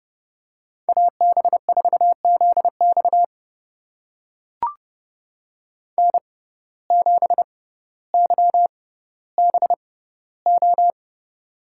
Morse code, heard as a keyed tone, about 30 wpm, spaced as for 15 wpm, 700 Hz. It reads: AB4ZX N7YBO
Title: USA Call Signs - 5 Characters - 6x Character Spacing - 30wpm